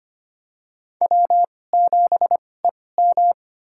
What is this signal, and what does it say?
Morse code, heard as a keyed tone, about 25 wpm, spaced as for 18 wpm, 700 Hz. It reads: W7EM
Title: USA Call Signs - 2x Character Spacing - 25wpm